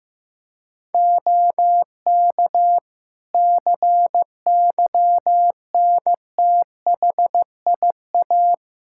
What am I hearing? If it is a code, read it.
Morse code, heard as a keyed tone, about 15 wpm, 700 Hz. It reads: OK CYNTHIA
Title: Practice QSO - Exchange 3 - More Info - 15wpm